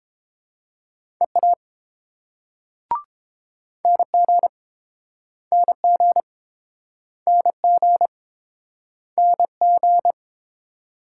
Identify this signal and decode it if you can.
Morse code, heard as a keyed tone, about 29 wpm, 700 Hz. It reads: EA NG NG NG NG